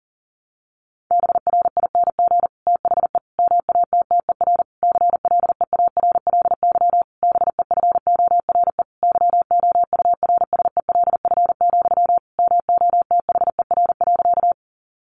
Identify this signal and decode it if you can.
Morse code, heard as a keyed tone, about 40 wpm, 700 Hz. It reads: BRING THE MATTER CLEARLY BEFORE YOURSELF, MOTHER.
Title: Sentences from Top 1300 Words 40wpm